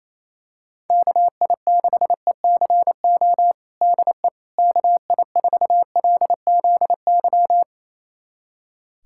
Morse code, heard as a keyed tone, about 28 wpm, 700 Hz. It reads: KI6ECO DE KI4LZY
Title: USA Callsign DE Callsign - 6 Characters - 28wpm